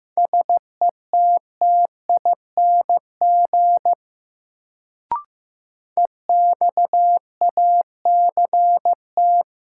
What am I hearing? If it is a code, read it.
Morse code, heard as a keyed tone, about 15 wpm, 700 Hz. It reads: SETTING EXACT
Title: Sentences from Top 1300 Words - New Word Drill Alt - 15wpm